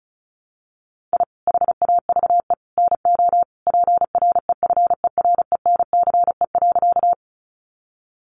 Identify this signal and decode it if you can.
Morse code, heard as a keyed tone, about 34 wpm, 700 Hz. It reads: I HAVE NO PREFERENCE.
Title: Sentences from Top 1500 Words 35wpm